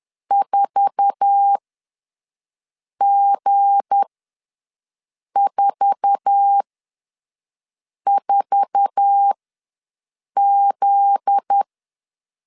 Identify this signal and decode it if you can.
Morse code, heard as a keyed tone, about 10 wpm, spaced as for 5 wpm, 800 Hz. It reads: 4G44Z
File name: LEZIONI / 8 YJ45 / 8 YJ4 ws 10wpm 4020